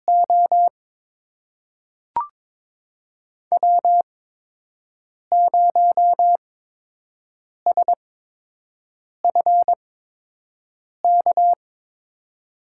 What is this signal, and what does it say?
Morse code, heard as a keyed tone, about 22 wpm, spaced as for 6 wpm, 700 Hz. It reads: O W0SFK
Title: USA Call Signs - 5 Characters - 8x Character Spacing - 22wpm